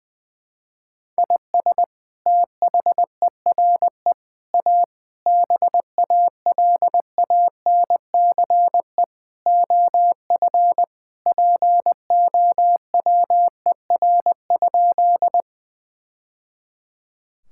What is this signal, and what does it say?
Morse code, heard as a keyed tone, about 19 wpm, 700 Hz. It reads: IS THERE A BALANCE OF POWER?
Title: Sentences from Top 1000 Words - Review Part 6 - 20wpm